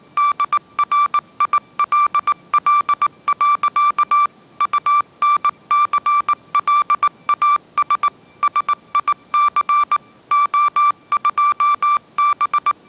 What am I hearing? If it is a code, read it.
Morse code, heard as a keyed tone, about 19 wpm, 1170 Hz. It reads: DRILL. UNCLAS SIC O2B
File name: 20wpm